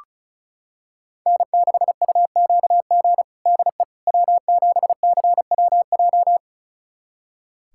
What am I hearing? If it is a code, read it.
Morse code, heard as a keyed tone, about 36 wpm, 700 Hz. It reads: N6UQG DE W7CWJ
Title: USA Callsign DE Callsign - 5 Characters - 35wpm